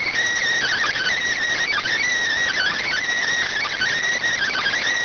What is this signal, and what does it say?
Emergency Alert System (SAME) tones, heard as harsh buzzing bursts. It is heard from 0.6 to 4.8 s.